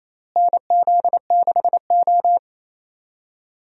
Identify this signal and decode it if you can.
Morse code, heard as a keyed tone, about 27 wpm, 700 Hz. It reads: NZ6O
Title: USA Call Signs - 4 Characters - Repeated 3x - 28wpm